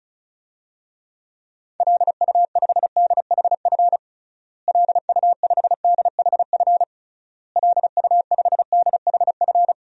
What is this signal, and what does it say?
Morse code, heard as a keyed tone, about 36 wpm, 700 Hz. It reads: LU5DHF LU5DHF LU5DHF
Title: DXCC-100 Call Signs - 6 Characters - Repeated 3x - 35wpm